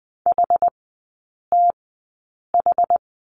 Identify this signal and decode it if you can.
Morse code, heard as a keyed tone, about 20 wpm, spaced as for 8 wpm, 700 Hz. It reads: HTH
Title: 060 Lesson 11e - Letter h - Three Characters 20wpm